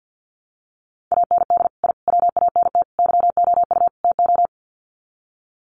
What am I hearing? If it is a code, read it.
Morse code, heard as a keyed tone, about 54 wpm, 700 Hz. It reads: AND I WANT YOU TO
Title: Sets of 5 Words Encore 50wpm